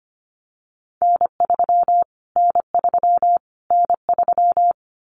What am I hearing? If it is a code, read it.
Morse code, heard as a keyed tone, about 25 wpm, 700 Hz. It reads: N3 N3 N3
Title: Callsign Prefixes for Top 25 DXCC - Repeated 3x - 25wpm